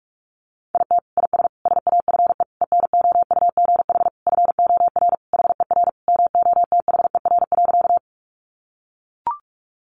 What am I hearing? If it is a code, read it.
Morse code, heard as a keyed tone, about 44 wpm, 700 Hz. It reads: IT IS SAFE ENOUGH FOR HER MOTHER.
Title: Sentences from Top 900 Words 45wpm